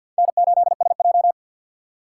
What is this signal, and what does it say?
Morse code, heard as a keyed tone, about 47 wpm, 700 Hz. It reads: N8SJ
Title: USA Call Signs - 6x Character Spacing - 50wpm